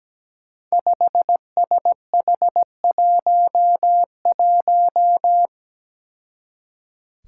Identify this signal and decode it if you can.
Morse code, heard as a keyed tone, about 17 wpm, 700 Hz. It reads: 5SH11
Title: Five Challenging Letter-Number Combinations 17wpm